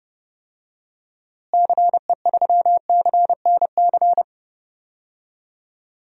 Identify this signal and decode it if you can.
Morse code, heard as a keyed tone, about 30 wpm, 700 Hz. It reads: CE3CNC
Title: DXCC-100 Call Signs - 6 Characters - Repeated 3x - 30wpm